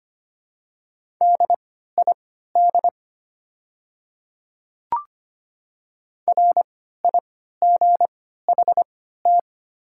Morse code, heard as a keyed tone, about 25 wpm, spaced as for 14 wpm, 690 Hz. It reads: DID RIGHT
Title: 3x Character Spacing - Top 200 Words - 25wpm